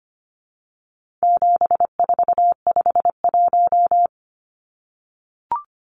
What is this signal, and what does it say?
Morse code, heard as a keyed tone, about 25 wpm, 700 Hz. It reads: 7451
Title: Numbers - 4-Digit 25wpm